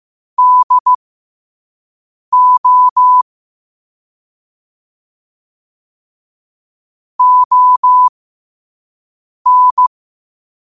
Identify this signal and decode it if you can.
Morse code, heard as a keyed tone, about 15 wpm, spaced as for 5 wpm, 1000 Hz. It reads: DO ON
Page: New quotes every day in morse code at 5 Words per minute.